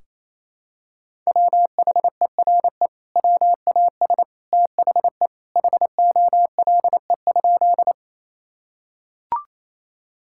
Morse code, heard as a keyed tone, about 27 wpm, 700 Hz. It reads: WHERE WAS THE HOLE?
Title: Sentences from Top 1600 Words 28wpm